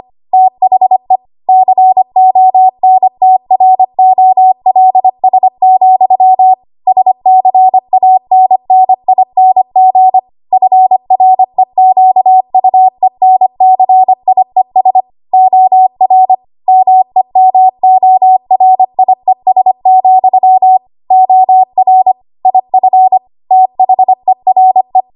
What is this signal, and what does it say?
Morse code, heard as a keyed tone, about 25 wpm, 750 Hz. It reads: THE CONTROLS, SCANNING FREQUENCIES OR MEMORIES, OR IF THERE
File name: Archive / 25 WPM / 25WPM